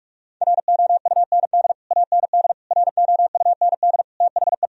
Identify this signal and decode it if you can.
Morse code, heard as a keyed tone, about 46 wpm, 700 Hz. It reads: ROUND AND ROUND THE
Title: Sentences from Top 1000 Words - Review Part 3 - 45wpm